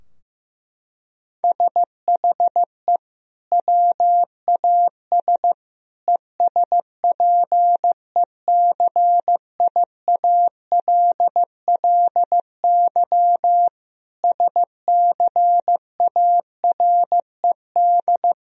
Morse code, heard as a keyed tone, about 15 wpm, 700 Hz. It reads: SHE WAS ESPECIALLY SCARED
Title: Sentences from Top 1000 Words - Review Part 4 - 15wpm